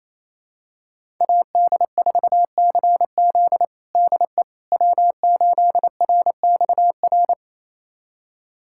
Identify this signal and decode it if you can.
Morse code, heard as a keyed tone, about 28 wpm, 700 Hz. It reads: AD4CZ DE W8RXR
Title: USA Callsign DE Callsign - 5 Characters - 28wpm